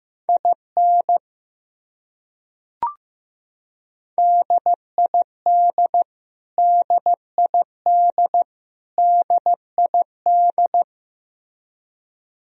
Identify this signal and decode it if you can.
Morse code, heard as a keyed tone, about 15 wpm, 700 Hz. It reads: IN DID DID DID
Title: Top 100 Words Repeated 15wpm